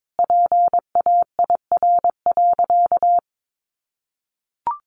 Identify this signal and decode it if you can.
Morse code, heard as a keyed tone, about 22 wpm, 700 Hz. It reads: PAIR.
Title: Sentences from Top 1200 Words 22wpm